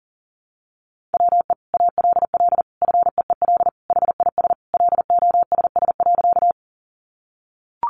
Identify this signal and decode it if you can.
Morse code, heard as a keyed tone, about 40 wpm, 700 Hz. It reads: WE ALL FEEL HIS LOSS.
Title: Sentences from Top 1400 Words 40wpm